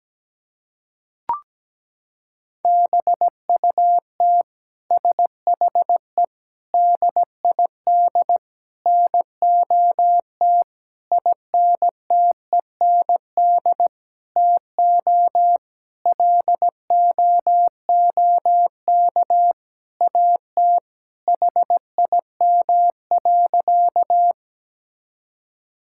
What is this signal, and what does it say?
Morse code, heard as a keyed tone, about 17 wpm, 700 Hz. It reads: BUT SHE DID NOT INTEND TO LOOK AT HIM.
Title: Sentences from Top 1600 Words 17wpm